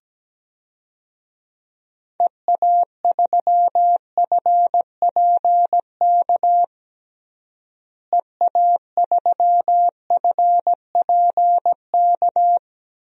Morse code, heard as a keyed tone, about 17 wpm, 700 Hz. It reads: EA3FPK EA3FPK
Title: DXCC-100 Call Signs - 6 Characters - Repeated 2x - 17wpm